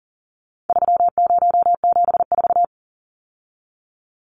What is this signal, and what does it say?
Morse code, heard as a keyed tone, about 40 wpm, 700 Hz. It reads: 3074
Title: Numbers - 4-Digit 40wpm